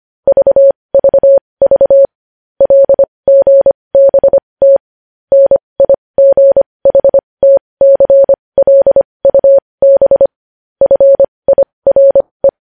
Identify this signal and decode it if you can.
Morse code, heard as a keyed tone, about 25 wpm, spaced as for 20 wpm, 550 Hz. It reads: VVV LGBT NIGHTCLUB FIRE
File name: podcast / News headlines-1x-25x20